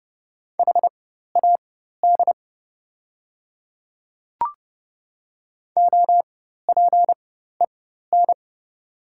Morse code, heard as a keyed tone, about 30 wpm, spaced as for 14 wpm, 700 Hz. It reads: HAD OPEN
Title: 4x Character Spacing - Top 300 Words - 30wpm